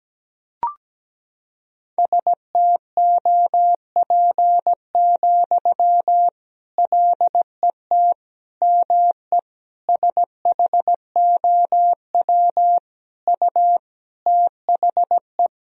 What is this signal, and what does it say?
Morse code, heard as a keyed tone, about 17 wpm, 700 Hz. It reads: STOP, LET ME SHOW U THE
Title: Sentences from Top 300 Words and Abbreviations 17wpm